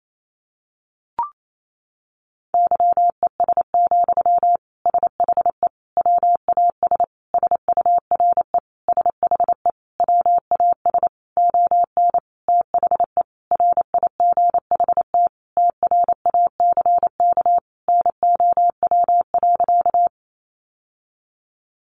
Morse code, heard as a keyed tone, about 28 wpm, 700 Hz. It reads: YES, SHE WAS SURE SHE WAS ON THE RIGHT TRACK NOW.
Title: Sentences from Top 1100 Words 28wpm